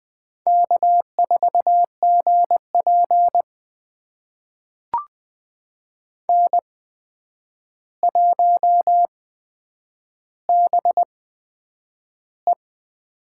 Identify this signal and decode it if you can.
Morse code, heard as a keyed tone, about 20 wpm, spaced as for 9 wpm, 700 Hz. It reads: K4GP N1BE
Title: USA Call Signs - 4 Characters - 8x Character Spacing - 20wpm